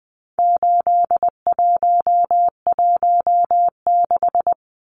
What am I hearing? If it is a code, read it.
Morse code, heard as a keyed tone, about 20 wpm, 700 Hz. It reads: 8116
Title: Numbers - 4-Digit 20wpm